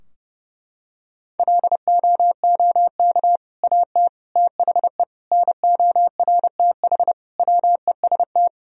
Morse code, heard as a keyed tone, about 30 wpm, 700 Hz. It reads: LOOK AT THE NORTH WEST
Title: Sentences from Top 1000 Words - Review Part 2 - 30wpm